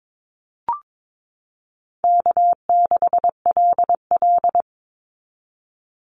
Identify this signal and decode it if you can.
Morse code, heard as a keyed tone, about 22 wpm, 700 Hz. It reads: K6LL
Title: Call Signs Repeated 3x - 22wpm